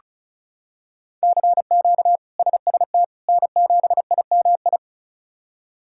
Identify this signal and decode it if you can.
Morse code, heard as a keyed tone, about 36 wpm, 700 Hz. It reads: CQ SST N7IMI
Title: K1USN SST QSO Practice - 35wpm